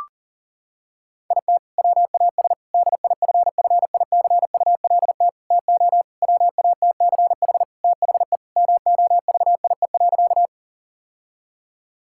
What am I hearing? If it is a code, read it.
Morse code, heard as a keyed tone, about 40 wpm, 700 Hz. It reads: IT WAS DIFFICULT TO WATCH THE MOVIE.
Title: Sentences from Top 1000 Words - Review Part 5 - 40wpm